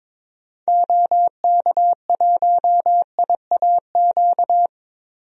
Morse code, heard as a keyed tone, about 22 wpm, 700 Hz. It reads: OK1IAQ
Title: DXCC-100 Call Signs - 6 Characters - 22wpm